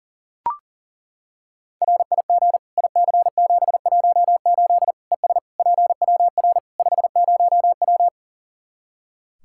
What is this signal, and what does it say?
Morse code, heard as a keyed tone, about 40 wpm, 700 Hz. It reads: RIG IC718 ES PWR 50W
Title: Practice QSO - Exchange 2 - Info - 40wpm